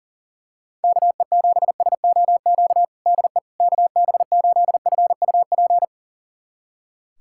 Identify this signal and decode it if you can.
Morse code, heard as a keyed tone, about 40 wpm, 700 Hz. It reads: KE7SOQ DE KB8FUP
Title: USA Callsign DE Callsign 40wpm